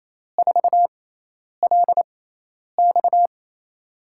Morse code, heard as a keyed tone, about 28 wpm, spaced as for 10 wpm, 700 Hz. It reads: 4LX
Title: USA Call Signs - 5 Characters - 6x Character Spacing - 28wpm